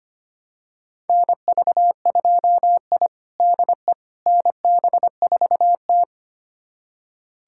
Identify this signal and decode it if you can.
Morse code, heard as a keyed tone, about 25 wpm, 700 Hz. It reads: NV2I DE NB4T
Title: USA Callsign DE Callsign - 4 Characters - 25wpm